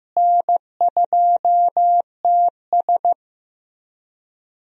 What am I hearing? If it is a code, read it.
Morse code, heard as a keyed tone, about 16 wpm, 700 Hz. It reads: N2TS
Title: USA Call Signs - 4 Characters - Repeated 3x - 15wpm